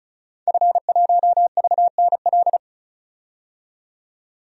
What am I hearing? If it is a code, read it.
Morse code, heard as a keyed tone, about 36 wpm, 700 Hz. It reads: F1VNL
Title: DXCC-100 Call Signs - 5 Characters - 35wpm